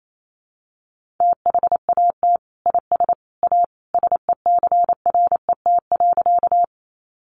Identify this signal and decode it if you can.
Morse code, heard as a keyed tone, about 28 wpm, 700 Hz. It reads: THAT IS A SECRET.
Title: Sentences from Top 1200 Words 28wpm